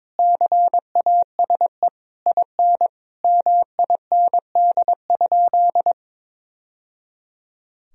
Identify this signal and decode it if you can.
Morse code, heard as a keyed tone, about 22 wpm, 700 Hz. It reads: CASE IN MIND?
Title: Sentences from Top 1000 Words - Review Part 4 - 22wpm